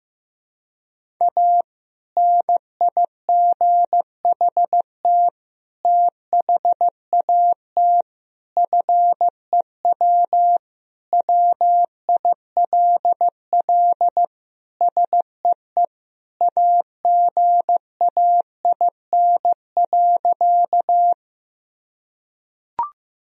Morse code, heard as a keyed tone, about 15 wpm, 700 Hz. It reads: A NIGHT THAT FEW WILL SEE AGAIN.
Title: Sentences from Top 1000 Words - Review Part 1 - 15wpm